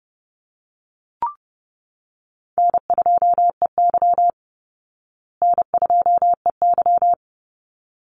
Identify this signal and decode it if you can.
Morse code, heard as a keyed tone, about 30 wpm, 700 Hz. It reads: N2EY N2EY
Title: Call Signs Repeated 3x - 30wpm